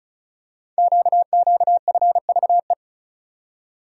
Morse code, heard as a keyed tone, about 36 wpm, 690 Hz. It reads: QQFVE
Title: Code Groups - 5 Letter:Numbers - 35wpm